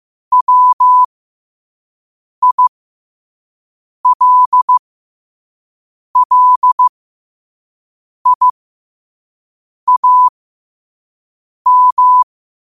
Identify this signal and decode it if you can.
Morse code, heard as a keyed tone, about 15 wpm, spaced as for 5 wpm, 1000 Hz. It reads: WILLIAM
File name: qotdfiles / 5wpm